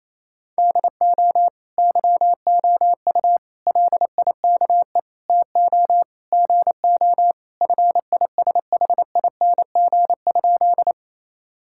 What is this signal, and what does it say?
Morse code, heard as a keyed tone, about 28 wpm, 700 Hz. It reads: DO YOU LIKE TO GO FISHING?
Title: Sentences from Top 1000 Words - Review Part 5 - 28wpm